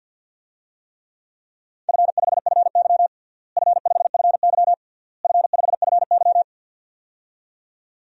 Morse code, heard as a keyed tone, about 51 wpm, 700 Hz. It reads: F5FY F5FY F5FY
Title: DXCC-100 Call Signs - 4 Characters - Repeated 3x - 50wpm